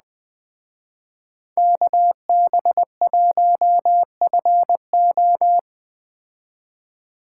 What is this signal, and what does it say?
Morse code, heard as a keyed tone, about 20 wpm, 700 Hz. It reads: KB1FO
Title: Speed Racing - USA 2x2 Call Signs 17wpm